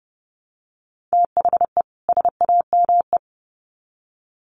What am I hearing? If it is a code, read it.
Morse code, heard as a keyed tone, about 30 wpm, 700 Hz. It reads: THE SAME
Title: Sets of 2 Words 30wpm